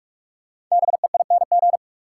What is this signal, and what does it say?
Morse code, heard as a keyed tone, about 44 wpm, 700 Hz. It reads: BEING
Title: Sentences from Top 600 Words - New Word Drill - 45wpm